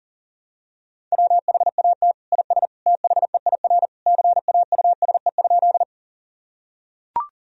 Morse code, heard as a keyed tone, about 40 wpm, 700 Hz. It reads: WHAT IS THEIR CAUSE?
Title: Sentences from Top 1000 Words - Review Part 1 - 40wpm